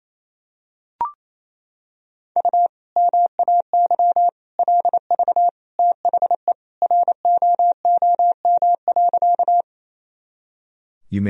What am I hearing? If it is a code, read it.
Morse code, heard as a keyed tone, about 28 wpm, 700 Hz. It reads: U MAY LV THE ROOM.
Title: Sentences from Top 400 Words and Abbreviations 28wpm